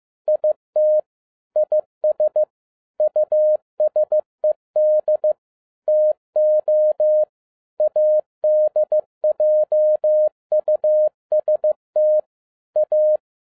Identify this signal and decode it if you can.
Morse code, heard as a keyed tone, about 15 wpm, 610 Hz. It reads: IT IS USED TO ADJUST A
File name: 2012-01-mp3 / n07-ARA JR0DIL-15wpm